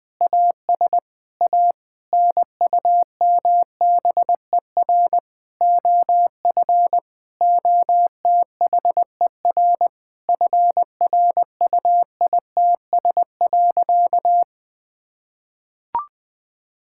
Morse code, heard as a keyed tone, about 20 wpm, 700 Hz. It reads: AS A NUMBER OF OTHER FRUITS.
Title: Sentences with 7-Word Phrases - 20wpm